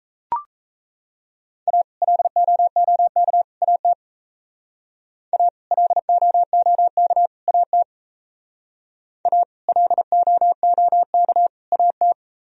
Morse code, heard as a keyed tone, about 34 wpm, 700 Hz. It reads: A LOOK AT A LOOK AT A LOOK AT
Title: Speed-Racing - Sets of 3 Words 28wpm